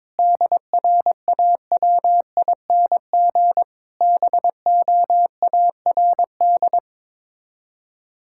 Morse code, heard as a keyed tone, about 22 wpm, 700 Hz. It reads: DRAWING BOARD
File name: practice / 22wpm / Sets of 7 Words - 22wpm